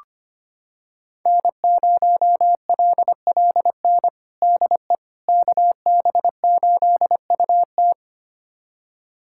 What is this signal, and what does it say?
Morse code, heard as a keyed tone, about 25 wpm, 700 Hz. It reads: N0LLN DE KB8UT
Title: USA Callsign DE Callsign - 5 Characters - 25wpm